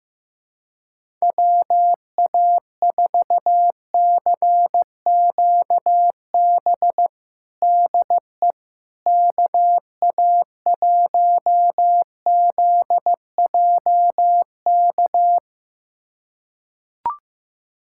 Morse code, heard as a keyed tone, about 16 wpm, 700 Hz. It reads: WA4CQB DE KA1ZJK
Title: USA Callsign DE Callsign - 6 Characters - 15wpm